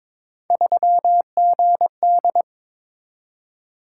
Morse code, heard as a keyed tone, about 22 wpm, 700 Hz. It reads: 3GD
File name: practice / 22wpm / Three Letter-Number Combinations - 22wpm